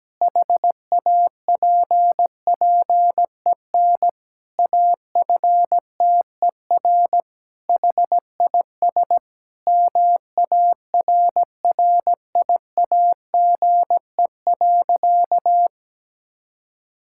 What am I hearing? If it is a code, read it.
Morse code, heard as a keyed tone, about 17 wpm, 700 Hz. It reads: HAPPEN AFTER HIS MARRIAGE.
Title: Sentences from Top 1000 Words - Review Part 5 - 17wpm